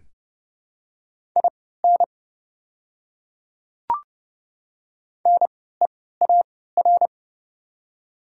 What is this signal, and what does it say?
Morse code, heard as a keyed tone, about 30 wpm, spaced as for 17 wpm, 700 Hz. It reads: IN NEAR
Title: 3x Character Spacing - Top 300 Words - 30wpm